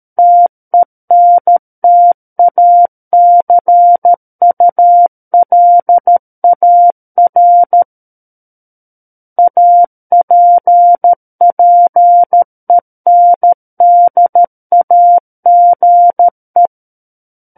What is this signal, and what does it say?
Morse code, heard as a keyed tone, about 13 wpm, 700 Hz. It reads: TENTACULAR APPENDAGE
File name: Book One / 04-Chapter 4 - 13 WPM